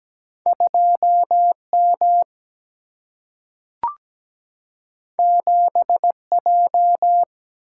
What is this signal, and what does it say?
Morse code, heard as a keyed tone, about 17 wpm, 700 Hz. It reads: 2M 7J
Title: Single-Letter Single-Number 17wpm